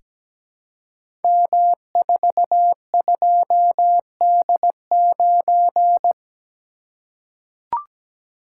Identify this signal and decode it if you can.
Morse code, heard as a keyed tone, about 17 wpm, 690 Hz. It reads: M42D9
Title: Code Groups - 5 Letter:Numbers - 17wpm